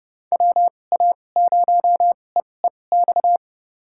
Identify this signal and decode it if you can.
Morse code, heard as a keyed tone, about 30 wpm, spaced as for 22 wpm, 700 Hz. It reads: WA0EEX
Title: USA Call Signs - 6 Characters - 2x Character Spacing - 30wpm